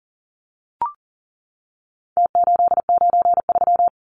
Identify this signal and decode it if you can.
Morse code, heard as a keyed tone, about 40 wpm, 700 Hz. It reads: T893
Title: Numbers - Contest Style - 4-Digit 40wpm